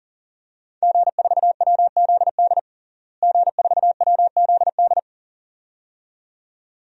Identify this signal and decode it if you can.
Morse code, heard as a keyed tone, about 37 wpm, 700 Hz. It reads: G4WZD G4WZD
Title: DXCC-100 Call Signs - 5 Characters - Repeated 2x - 40wpm